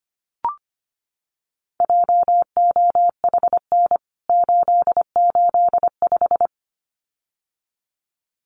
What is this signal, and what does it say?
Morse code, heard as a keyed tone, about 25 wpm, 700 Hz. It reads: JOHN 885
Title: CWT Exchange 25wpm